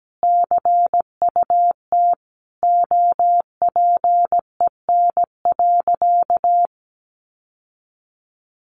Morse code, heard as a keyed tone, about 17 wpm, 700 Hz. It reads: CUT OPEN.
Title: Sentences from Top 1300 Words 17wpm